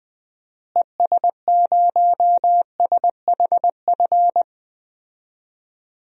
Morse code, heard as a keyed tone, about 20 wpm, 700 Hz. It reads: ES0SHF
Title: DXCC-100 Call Signs - 6 Characters - Repeated 3x - 20wpm